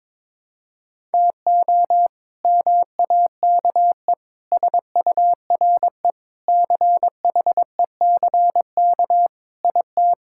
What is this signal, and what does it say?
Morse code, heard as a keyed tone, about 22 wpm, 700 Hz. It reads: TO MAKE SURE CHECK IT
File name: practice / 22wpm / Sentences with 3-Word Phrases - 22wpm